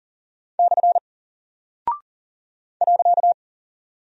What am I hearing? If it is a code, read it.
Morse code, heard as a keyed tone, about 40 wpm, 700 Hz. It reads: / .
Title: Single Character (Letter:Number:Punctuation:Prosign) 40wpm